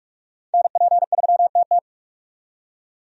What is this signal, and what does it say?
Morse code, heard as a keyed tone, about 46 wpm, 700 Hz. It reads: NP3TT
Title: USA Call Signs - 5 Characters - 45wpm